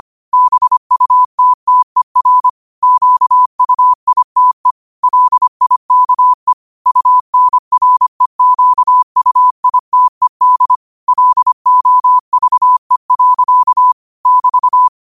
Morse code, heard as a keyed tone, about 25 wpm, 1000 Hz. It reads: BUTTER QUITE LIKE UNREQUITED LOVE. <BT>
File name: qotdfiles / 25wpm